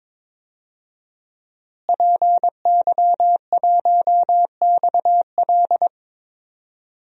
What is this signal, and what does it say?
Morse code, heard as a keyed tone, about 21 wpm, 700 Hz. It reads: PY1XL
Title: DXCC-100 Call Signs - 5 Characters - Repeated 2x - 22wpm